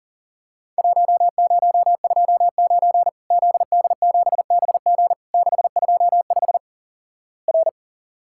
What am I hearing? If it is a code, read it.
Morse code, heard as a keyed tone, about 37 wpm, 700 Hz. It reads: 1029 ZD7BG 625 R
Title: Worked All Europe DX Contest - QTC Exchange - 40wpm